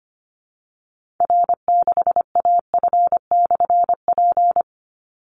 Morse code, heard as a keyed tone, about 25 wpm, 700 Hz. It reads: R6AF/P
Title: Call Signs Repeated 2x - 25wpm